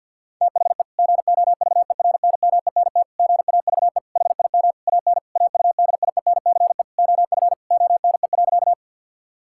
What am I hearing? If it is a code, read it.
Morse code, heard as a keyed tone, about 50 wpm, 700 Hz. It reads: THE GOVERNMENT GAVE HIM AN AUDIENCE OF ONE.
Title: Sentences from Top 1000 Words - Review Part 5 - 50wpm